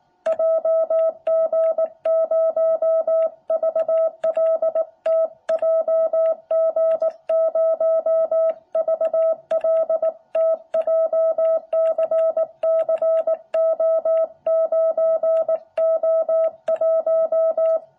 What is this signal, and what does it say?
Morse code, heard as a keyed tone, about 18 wpm, 660 Hz. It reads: JG0VLT JG0VLT JCCO9O1
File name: jh8cbh / 2021cw